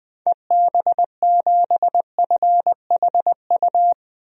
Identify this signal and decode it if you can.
Morse code, heard as a keyed tone, about 20 wpm, 700 Hz. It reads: EB7FHU
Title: DXCC-100 Call Signs - 6 Characters - 20wpm